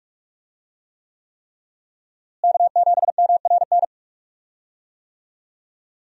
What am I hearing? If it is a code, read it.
Morse code, heard as a keyed tone, about 46 wpm, 700 Hz. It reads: K7MRN
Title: USA Call Signs - 5 Characters - Repeated 3x - 45wpm